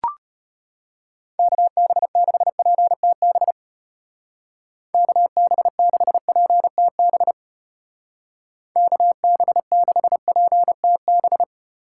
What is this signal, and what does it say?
Morse code, heard as a keyed tone, about 30 wpm, 700 Hz. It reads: KB6PTB KB6PTB KB6PTB
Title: Speed Racing - USA 2x3 Call Signs 25wpm